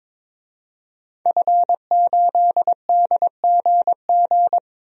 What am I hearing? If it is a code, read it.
Morse code, heard as a keyed tone, about 22 wpm, 700 Hz. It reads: F8DGG
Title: DXCC-100 Call Signs - 5 Characters - 22wpm